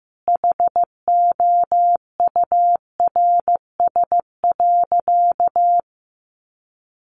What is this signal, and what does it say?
Morse code, heard as a keyed tone, about 15 wpm, 700 Hz. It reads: HOURS.
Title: Additional Easy Phrases 15wpm